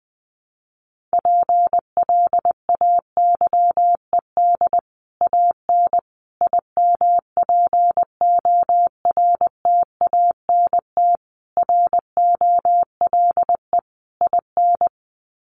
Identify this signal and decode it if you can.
Morse code, heard as a keyed tone, about 20 wpm, 700 Hz. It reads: PLAYED AN IMPORTANT ROLE IN
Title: Sets of 5 Words Encore2 20wpm